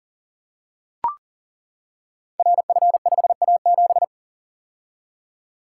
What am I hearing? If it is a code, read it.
Morse code, heard as a keyed tone, about 42 wpm, 690 Hz. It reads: RF5A7
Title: Code Groups - 5 Letter:Numbers - 40wpm